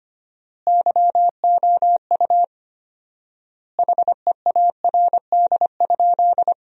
Morse code, heard as a keyed tone, about 25 wpm, 700 Hz. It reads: YOU HEARD?
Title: 4x Word-Spacing - Sentences from Top 400 Words - 25wpm